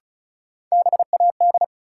Morse code, heard as a keyed tone, about 36 wpm, 700 Hz. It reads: BAD
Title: Sentences from Top 700 Words - New Word Drill - 35wpm